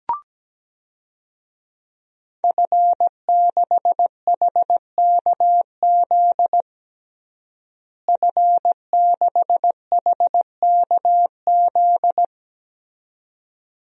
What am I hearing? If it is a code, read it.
Morse code, heard as a keyed tone, about 17 wpm, 700 Hz. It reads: F6HKZ F6HKZ
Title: DXCC-100 Call Signs - 5 Characters - Repeated 2x - 17wpm